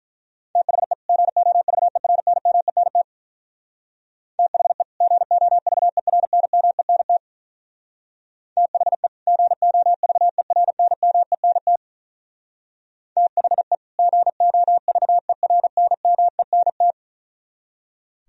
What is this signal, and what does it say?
Morse code, heard as a keyed tone, about 43 wpm, 700 Hz. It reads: THE GOVERNMENT THE GOVERNMENT THE GOVERNMENT THE GOVERNMENT
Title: Speed-Racing - Sets of 2 Words 35wpm